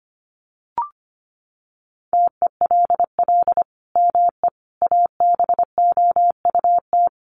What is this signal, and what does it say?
Morse code, heard as a keyed tone, about 25 wpm, 700 Hz. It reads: TELL ME ABOUT
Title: Sentences from Top 1500 Words 25wpm